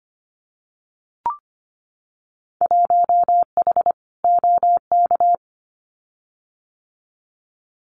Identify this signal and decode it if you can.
Morse code, heard as a keyed tone, about 25 wpm, 700 Hz. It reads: 1H OK
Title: Winter Field Day Exchanges 25wpm